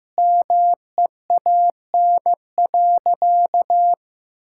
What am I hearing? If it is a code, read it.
Morse code, heard as a keyed tone, about 15 wpm, 700 Hz. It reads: MEAN.
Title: Sentences from Top 1000 Words - Review Part 6 - 15wpm